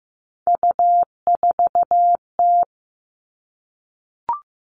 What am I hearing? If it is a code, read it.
Morse code, heard as a keyed tone, about 15 wpm, 700 Hz. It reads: U4T
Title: All Cut Numbers - 3-Digit 15wpm